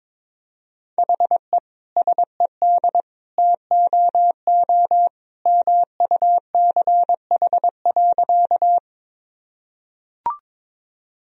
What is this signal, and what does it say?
Morse code, heard as a keyed tone, about 22 wpm, 700 Hz. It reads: HE SED TOO MUCH.
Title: Sentences from Top 200 Words and Abbreviations 22wpm